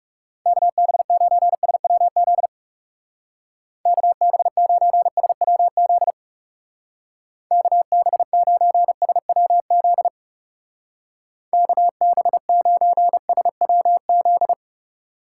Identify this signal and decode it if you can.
Morse code, heard as a keyed tone, about 36 wpm, 700 Hz. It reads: KB9SWZ KB9SWZ KB9SWZ KB9SWZ
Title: Speed Racing - USA Call Signs 30wpm